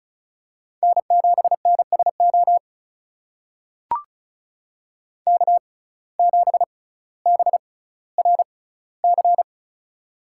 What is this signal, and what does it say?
Morse code, heard as a keyed tone, about 36 wpm, spaced as for 18 wpm, 700 Hz. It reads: N7NSO K7BRC
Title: USA Call Signs - 5 Characters - 6x Character Spacing - 35wpm